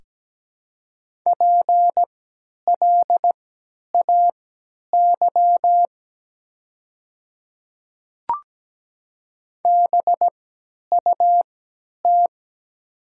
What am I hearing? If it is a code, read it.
Morse code, heard as a keyed tone, about 17 wpm, spaced as for 10 wpm, 700 Hz. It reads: PLAY BUT
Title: 3x Character Spacing - Top 500 Words - 17wpm